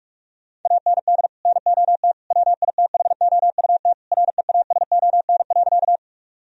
Morse code, heard as a keyed tone, about 46 wpm, 700 Hz. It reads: AND NOT WITHOUT REASON.
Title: Sentences from Top 1000 Words - Review Part 3 - 45wpm